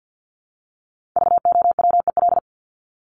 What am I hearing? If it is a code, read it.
Morse code, heard as a keyed tone, about 50 wpm, 700 Hz. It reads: VOWEL
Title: Top 500 Words 50wpm